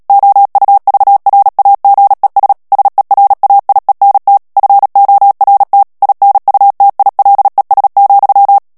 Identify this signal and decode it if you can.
Morse code, heard as a keyed tone, about 37 wpm, 800 Hz. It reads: OUVRAGES SERAIENT FORT INUTILES,
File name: CW / FICHIERS-MP3 / FRANCAIS TEXTE CLAIR 2 VITESSE 35 MOTS MINUTES